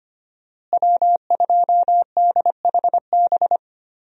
Morse code, heard as a keyed tone, about 25 wpm, 700 Hz. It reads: W2DHB
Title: USA Call Signs - 5 Characters - Repeated 3x - 25wpm